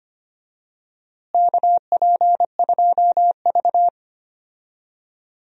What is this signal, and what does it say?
Morse code, heard as a keyed tone, about 25 wpm, 700 Hz. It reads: KP2V
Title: DXCC-100 Call Signs - 4 Characters - Repeated 3x - 25wpm